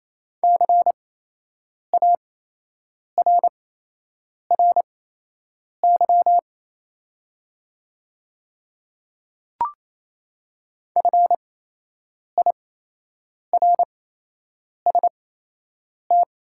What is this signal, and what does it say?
Morse code, heard as a keyed tone, about 28 wpm, spaced as for 8 wpm, 690 Hz. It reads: CARRY FIRST
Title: 8x Character Spacing - Top 300 Words - 28wpm